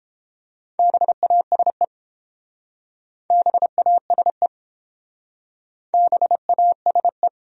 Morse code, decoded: BASE BASE BASE

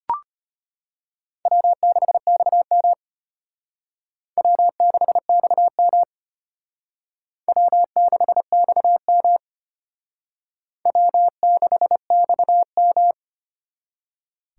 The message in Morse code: W6XM W6XM W6XM W6XM

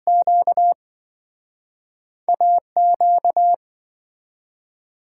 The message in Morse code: QA Q

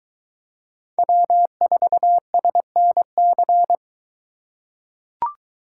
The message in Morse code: W4SNC

